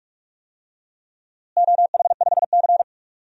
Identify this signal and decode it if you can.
Morse code, heard as a keyed tone, about 46 wpm, 700 Hz. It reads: OH5C
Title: DXCC-100 Call Signs - 4 Characters - Repeated 3x - 45wpm